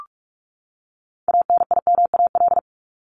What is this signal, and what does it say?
Morse code, heard as a keyed tone, about 44 wpm, 700 Hz. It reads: ANIMAL
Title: Top 200 Words 45wpm